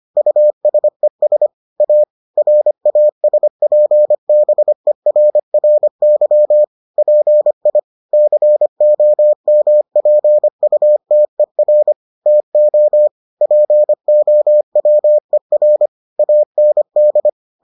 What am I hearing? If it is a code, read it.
Morse code, heard as a keyed tone, about 25 wpm, 600 Hz. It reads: USES A RASPBERRY PI COMPUTER TO POWER AND